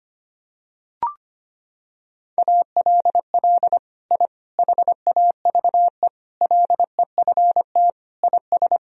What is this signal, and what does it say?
Morse code, heard as a keyed tone, about 25 wpm, 700 Hz. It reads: ALL I HAVE LEFT IS
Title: Sentences from Top 1000 Words - Review Part 3 - 25wpm